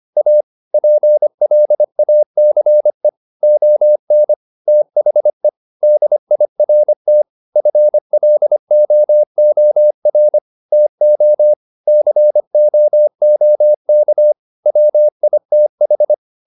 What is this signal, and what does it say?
Morse code, heard as a keyed tone, about 25 wpm, 600 Hz. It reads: A PLACE ON THE DIRT FLOOR TO COOK WITH
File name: the old man and the sea part 1 25wpm